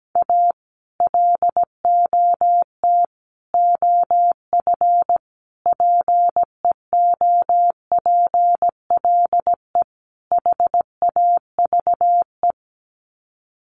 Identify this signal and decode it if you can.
Morse code, heard as a keyed tone, about 17 wpm, 700 Hz. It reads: A LOT OF PEOPLE HAVE
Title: Sets of 5 Words 17wpm